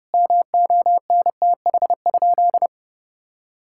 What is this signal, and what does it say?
Morse code, heard as a keyed tone, about 30 wpm, 700 Hz. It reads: MONTH?
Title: Sentences from Top 1000 Words - Review Part 5 - 30wpm